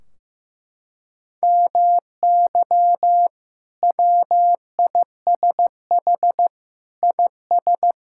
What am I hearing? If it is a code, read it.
Morse code, heard as a keyed tone, about 15 wpm, 700 Hz. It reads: MY WISH IS
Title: Sentences from Top 1600 Words 15wpm